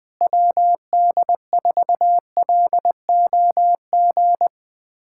Morse code, heard as a keyed tone, about 19 wpm, 700 Hz. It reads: WD4LOG
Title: USA Callsign DE Callsign - 6 Characters - 20wpm